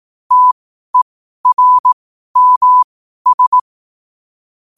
Morse code, heard as a keyed tone, about 18 wpm, spaced as for 12 wpm, 1000 Hz. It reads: TERMS
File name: 12wpm